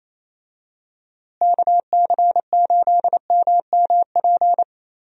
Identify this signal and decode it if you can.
Morse code, heard as a keyed tone, about 27 wpm, 700 Hz. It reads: KC8MMP
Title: USA Call Signs - 6 Characters - Repeated 3x - 28wpm